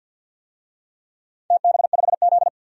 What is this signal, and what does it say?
Morse code, heard as a keyed tone, about 47 wpm, 700 Hz. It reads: TB5Z
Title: DXCC-100 Call Signs - 4 Characters - 50wpm